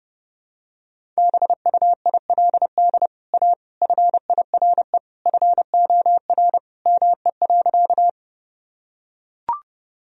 Morse code, heard as a keyed tone, about 30 wpm, 700 Hz. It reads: BUILD A FIRE FOR ME.
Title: Sentences from Top 1000 Words - Review Part 1 - 30wpm